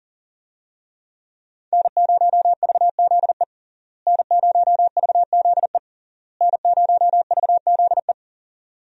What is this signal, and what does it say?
Morse code, heard as a keyed tone, about 37 wpm, 700 Hz. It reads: N0VZE N0VZE N0VZE
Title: USA Call Signs - 5 Characters - Repeated 3x - 40wpm